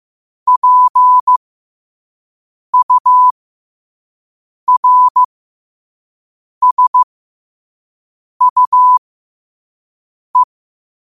Morse code, heard as a keyed tone, about 15 wpm, spaced as for 5 wpm, 1000 Hz. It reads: PURSUE
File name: qotdfiles / 5wpm